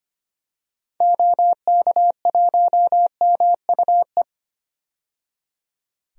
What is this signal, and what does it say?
Morse code, heard as a keyed tone, about 25 wpm, 700 Hz. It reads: OK1MUE
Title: DXCC-100 Call Signs - 6 Characters - 25wpm